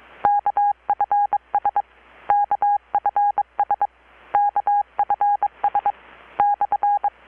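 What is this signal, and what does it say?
Morse code, heard as a keyed tone, about 22 wpm, 820 Hz. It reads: KFS KFS KFS/
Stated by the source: Morse from KPH's Night of Nights XX maritime radiotelegraphy event, received in Chiapas, Mexico about 3,800 km away on a Kenwood R-600 with a grounded 10 m long-wire antenna.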